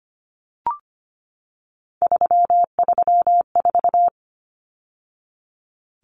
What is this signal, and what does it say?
Morse code, heard as a keyed tone, about 25 wpm, 700 Hz. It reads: 334
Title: Numbers - 3-Digit 25wpm